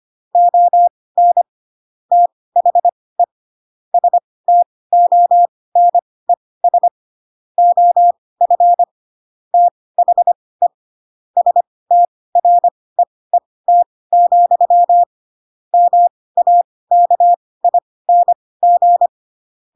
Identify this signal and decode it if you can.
Morse code, heard as a keyed tone, about 25 wpm, spaced as for 18 wpm, 700 Hz. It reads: ON THE STONES OF THE STREET, MAKING
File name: Pinocchio ch3 12 25 ik1hge